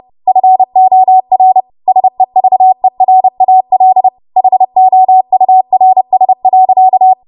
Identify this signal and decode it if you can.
Morse code, heard as a keyed tone, about 30 wpm, 750 Hz. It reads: FOR SEVERAL HOURS.